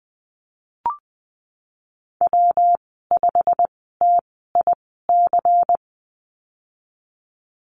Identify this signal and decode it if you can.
Morse code, heard as a keyed tone, about 20 wpm, spaced as for 15 wpm, 700 Hz. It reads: W5TIC
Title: 128 Lesson 21g - Number 6 - Call Signs 20wpm